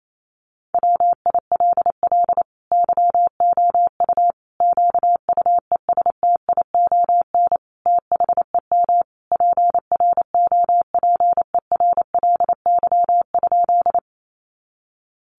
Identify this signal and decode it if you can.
Morse code, heard as a keyed tone, about 28 wpm, 700 Hz. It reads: WILL YOU QUESTION THEM PROPERLY?
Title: Sentences from Top 1100 Words 28wpm